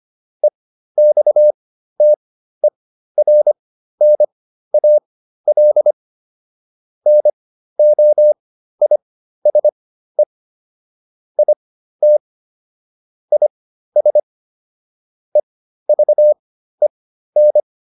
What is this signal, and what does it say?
Morse code, heard as a keyed tone, about 25 wpm, spaced as for 13 wpm, 600 Hz. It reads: EXTERNAL NOISE IT IS EVEN